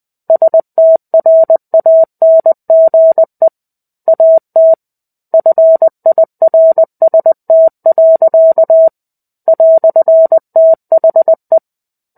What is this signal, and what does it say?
Morse code, heard as a keyed tone, about 20 wpm, 650 Hz. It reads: STRANGE AT FIRST. "THE
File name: 05-Chapter-05 20 WPM